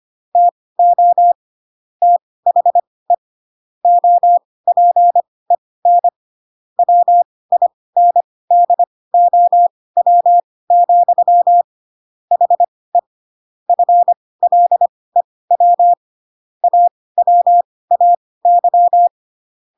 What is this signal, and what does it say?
Morse code, heard as a keyed tone, about 25 wpm, spaced as for 18 wpm, 700 Hz. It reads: TO THE OPEN WINDOW, HE FLEW AWAY